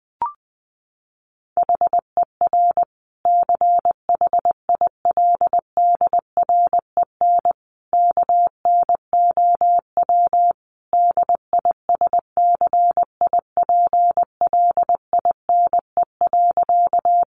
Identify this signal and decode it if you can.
Morse code, heard as a keyed tone, about 19 wpm, 700 Hz. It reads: HER CHILDREN KNOW DISCIPLINE.
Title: Sentences from Top 900 Words 20wpm